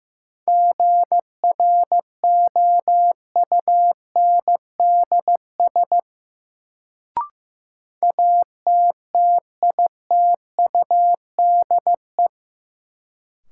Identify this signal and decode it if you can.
Morse code, heard as a keyed tone, about 15 wpm, 700 Hz. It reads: GROUNDS ATTITUDE
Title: Sentences from Top 1300 Words - New Word Drill - 15wpm